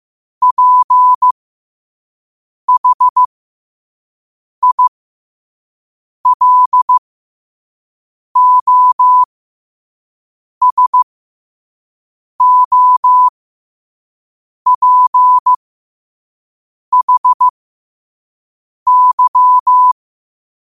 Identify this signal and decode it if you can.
Morse code, heard as a keyed tone, about 15 wpm, spaced as for 5 wpm, 1000 Hz. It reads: PHILOSOPHY
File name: 5wpm